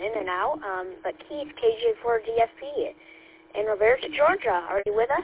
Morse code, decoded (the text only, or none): none